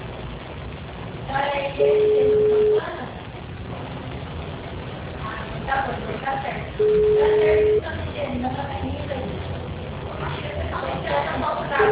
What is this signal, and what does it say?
Ringback tone (Europe, 425 Hz), with speech.
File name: out- -2014- - - .14140